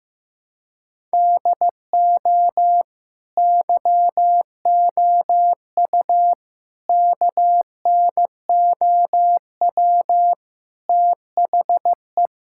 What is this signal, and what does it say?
Morse code, heard as a keyed tone, about 15 wpm, 700 Hz. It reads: DO YOU KNOW THE
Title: Sentences from Top 1000 Words - Review Part 6 - 15wpm